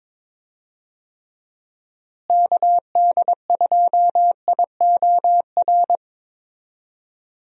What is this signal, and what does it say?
Morse code, heard as a keyed tone, about 22 wpm, 700 Hz. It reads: KD2IOR
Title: USA Call Signs - 6 Characters - Repeated 2x - 22wpm